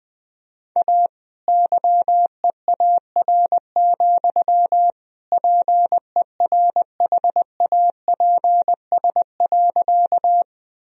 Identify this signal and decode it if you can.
Morse code, heard as a keyed tone, about 20 wpm, 700 Hz. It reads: A YEAR, PERHAPS.
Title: Sentences from Top 1000 Words - Review Part 1 - 20wpm